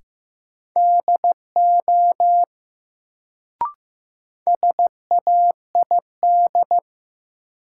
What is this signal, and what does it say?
Morse code, heard as a keyed tone, about 15 wpm, 700 Hz. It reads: DO SAID
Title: Sentences from Top 100 Words - New Word Drill - 15wpm